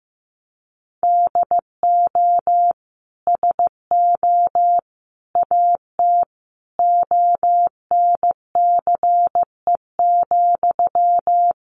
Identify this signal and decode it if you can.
Morse code, heard as a keyed tone, about 15 wpm, 700 Hz. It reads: DO SO AT ONCE,
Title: Sentences from Top 1500 Words 15wpm